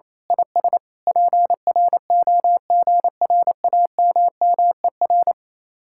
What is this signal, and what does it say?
Morse code, heard as a keyed tone, about 27 wpm, 700 Hz. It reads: IS PROGRAMMER
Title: Speed-Racing Common QSO Phrases - 28wpm